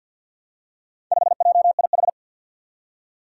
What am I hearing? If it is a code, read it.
Morse code, heard as a keyed tone, about 52 wpm, 700 Hz. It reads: 5JIH
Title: Four Challenging Letter-Number Combinations 50wpm